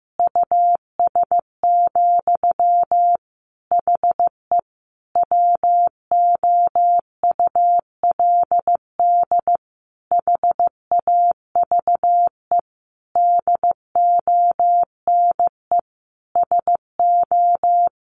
Morse code, heard as a keyed tone, about 15 wpm, 700 Hz. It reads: US, HE WOULD HAVE DONE SO
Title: Sentences from Top 1500 Words 15wpm